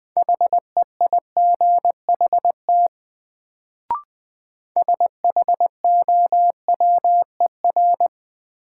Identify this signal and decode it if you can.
Morse code, heard as a keyed tone, about 20 wpm, 700 Hz. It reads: HEIGHT SHOWER
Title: Sentences from Top 1600 Words - New Word Drill - 20wpm